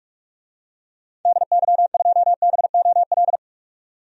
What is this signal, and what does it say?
Morse code, heard as a keyed tone, about 45 wpm, 700 Hz. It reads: DY2BOL
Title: DXCC-100 Call Signs - 6 Characters - 45wpm